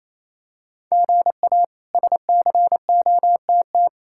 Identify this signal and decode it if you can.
Morse code, heard as a keyed tone, about 28 wpm, 700 Hz. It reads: GA SCOTT
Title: K1USN SST QSO Practice - 28wpm